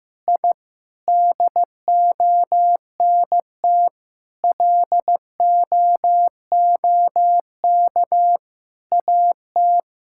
Morse code, heard as a keyed tone, about 15 wpm, 700 Hz. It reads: I DONT LOOK AT
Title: Sentences from Top 1000 Words - Review Part 5 - 15wpm